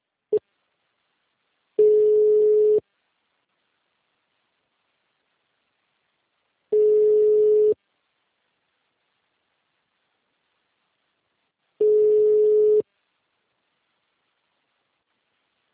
Ringback tone (Europe, 425 Hz).